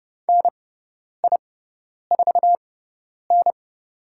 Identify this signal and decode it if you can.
Morse code, heard as a keyed tone, about 30 wpm, spaced as for 10 wpm, 700 Hz. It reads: NI4N
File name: Lesson 16g - Letter c - Call Signs 30wpm